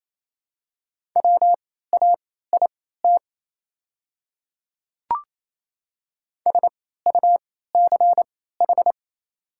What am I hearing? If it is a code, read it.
Morse code, heard as a keyed tone, about 28 wpm, spaced as for 16 wpm, 690 Hz. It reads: WAIT SUCH
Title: 3x Character Spacing - Top 500 Words - 28wpm